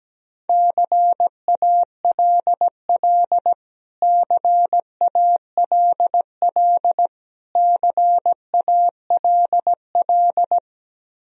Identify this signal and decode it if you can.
Morse code, heard as a keyed tone, about 17 wpm, 700 Hz. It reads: CALL CALL CALL
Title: Top 100 Words Repeated 17wpm